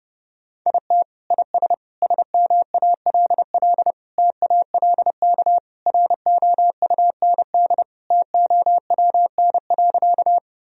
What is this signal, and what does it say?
Morse code, heard as a keyed tone, about 30 wpm, 700 Hz. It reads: IT IS SMALL TALK ROUND TOWN.
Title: Sentences from Top 1000 Words - Review Part 1 - 30wpm